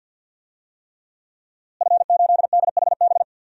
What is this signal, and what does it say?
Morse code, heard as a keyed tone, about 50 wpm, 700 Hz. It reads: F8DHB
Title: DXCC-100 Call Signs - 5 Characters - Repeated 3x - 50wpm